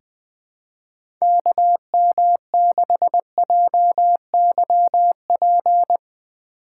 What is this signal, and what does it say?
Morse code, heard as a keyed tone, about 21 wpm, 700 Hz. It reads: KM6JYP
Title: USA Call Signs - 6 Characters - 2x Character Spacing - 20wpm